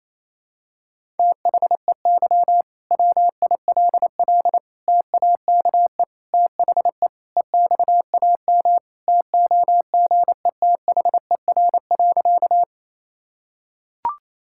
Morse code, heard as a keyed tone, about 27 wpm, 700 Hz. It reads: THEY WILL TAKE THE EXAM TOGETHER.
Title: Sentences from Top 1000 Words - Review Part 5 - 28wpm